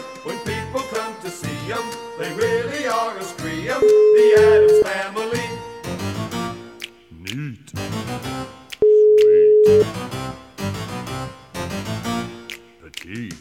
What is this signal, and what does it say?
Ringback tone (Europe, 425 Hz), with speech.